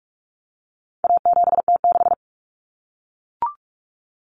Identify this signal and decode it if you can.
Morse code, heard as a keyed tone, about 44 wpm, 700 Hz. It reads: A7T6
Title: Common Cut Numbers - 4-Digit 45wpm